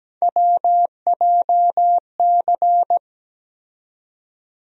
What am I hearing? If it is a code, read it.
Morse code, heard as a keyed tone, about 17 wpm, 700 Hz. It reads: WJC
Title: USA Call Sign Postfixes - 3 Letters 17wpm